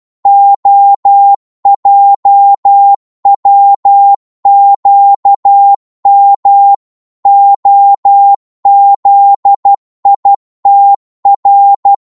Morse code, heard as a keyed tone, about 12 wpm, 800 Hz. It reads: OJWQM OZITR
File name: Morse / ltr-12wpm